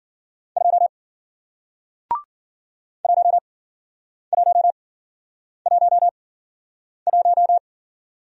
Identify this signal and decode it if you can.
Morse code, heard as a keyed tone, about 51 wpm, spaced as for 9 wpm, 700 Hz. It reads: 2 1111